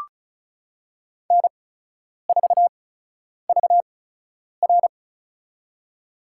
Morse code, heard as a keyed tone, about 36 wpm, spaced as for 10 wpm, 700 Hz. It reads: N4VR
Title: USA Call Signs - 8x Character Spacing - 35wpm